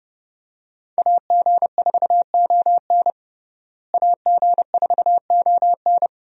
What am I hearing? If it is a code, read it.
Morse code, heard as a keyed tone, about 30 wpm, 700 Hz. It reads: AG4ON AG4ON
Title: USA Call Signs - 5 Characters - Repeated 2x - 30wpm